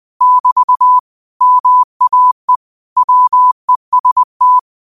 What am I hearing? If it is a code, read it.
Morse code, heard as a keyed tone, about 20 wpm, 1000 Hz. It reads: <BT> MAE WEST